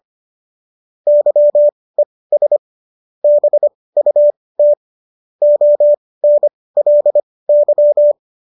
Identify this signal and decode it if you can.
Morse code, heard as a keyed tone, about 25 wpm, spaced as for 18 wpm, 590 Hz. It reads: YES BUT ONLY
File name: qso211 18